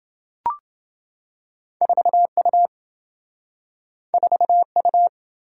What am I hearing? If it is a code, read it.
Morse code, heard as a keyed tone, about 27 wpm, 700 Hz. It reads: 4U 4U